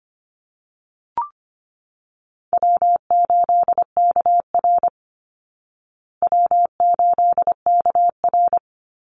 Morse code, heard as a keyed tone, about 25 wpm, 700 Hz. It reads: W8KR W8KR
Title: Easy Call Signs Repeated 2x - 25wpm